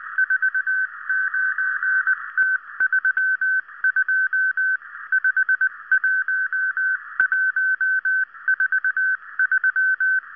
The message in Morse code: 49T32 51143